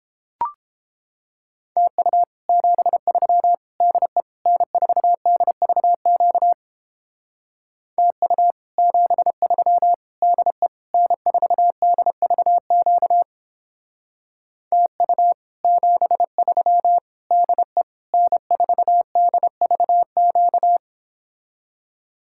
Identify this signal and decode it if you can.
Morse code, heard as a keyed tone, about 29 wpm, 700 Hz. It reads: TU 73 DE N4DVQ TU 73 DE N4DVQ TU 73 DE N4DVQ